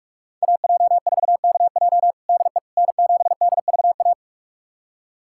Morse code, heard as a keyed tone, about 46 wpm, 700 Hz. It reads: AJ4KJ DE N7DVA
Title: USA Callsign DE Callsign - 5 Characters - 45wpm